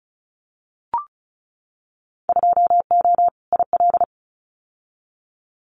Morse code, heard as a keyed tone, about 34 wpm, 700 Hz. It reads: 2O IL
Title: Winter Field Day Exchanges 35wpm